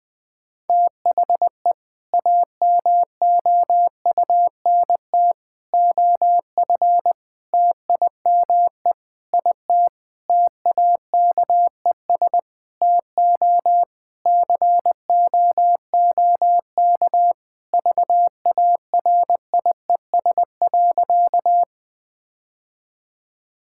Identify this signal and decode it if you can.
Morse code, heard as a keyed tone, about 21 wpm, 700 Hz. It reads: THE AMOUNT OF TIME IT TAKES TO COOK VARIES.
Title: Sentences with 7-Word Phrases - Spoken Beginning - 20wpm